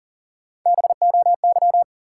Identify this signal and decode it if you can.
Morse code, heard as a keyed tone, about 40 wpm, 700 Hz. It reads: BOY
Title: Sentences from Top 200 Words - New Word Drill - 40wpm